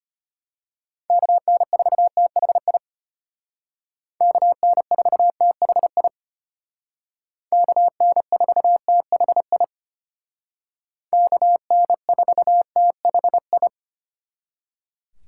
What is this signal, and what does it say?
Morse code, heard as a keyed tone, about 32 wpm, 700 Hz. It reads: KN4THI KN4THI KN4THI KN4THI